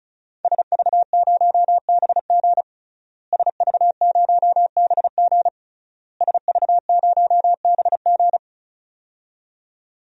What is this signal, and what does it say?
Morse code, heard as a keyed tone, about 35 wpm, 700 Hz. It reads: SV0BG SV0BG SV0BG
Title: DXCC-100 Call Signs - 5 Characters - Repeated 3x - 35wpm